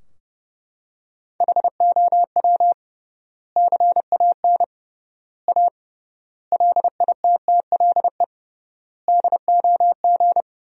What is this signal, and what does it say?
Morse code, heard as a keyed tone, about 30 wpm, 700 Hz. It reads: HOW CAN A LITTLE DOG
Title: 3x Word-Spacing - Sentences from Top 500 Words - 30wpm